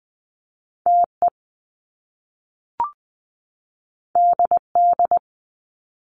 Morse code, heard as a keyed tone, about 20 wpm, 700 Hz. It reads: TE DD
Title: Top Two Letter Combinations in Words 20wpm